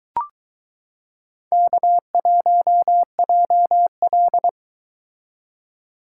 Morse code, heard as a keyed tone, about 23 wpm, 700 Hz. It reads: K1JL